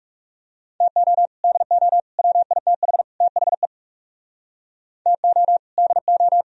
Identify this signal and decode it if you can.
Morse code, heard as a keyed tone, about 37 wpm, 700 Hz. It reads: TO DO WITH THE TO DO
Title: Speed-Racing - Sets of 4 Words 30wpm